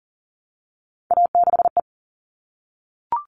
Morse code, heard as a keyed tone, about 40 wpm, 700 Hz. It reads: A6E
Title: All Cut Numbers - 3-Digit 40wpm